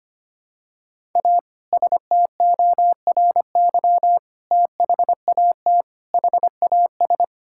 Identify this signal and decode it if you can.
Morse code, heard as a keyed tone, about 25 wpm, 700 Hz. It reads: A STORY THAT HAS
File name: practice / 25wpm / Sentences with 2-Word Phrases - 25wpm